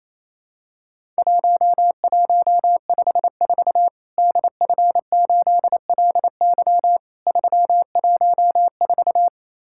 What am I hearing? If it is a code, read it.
Morse code, heard as a keyed tone, about 27 wpm, 700 Hz. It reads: 1154 DF8LY 314
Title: Worked All Europe DX Contest - QTC Exchange - 28wpm